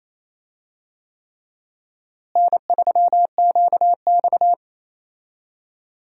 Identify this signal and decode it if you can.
Morse code, heard as a keyed tone, about 28 wpm, 700 Hz. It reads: N3QX
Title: USA Call Signs - 4 Characters - Repeated 3x - 28wpm